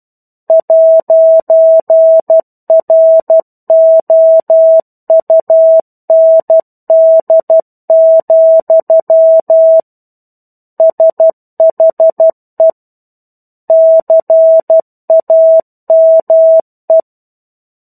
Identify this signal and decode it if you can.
Morse code, heard as a keyed tone, about 12 wpm, 650 Hz. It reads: 'ROUND, SHE CAME
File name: Chapter-01 12 WPM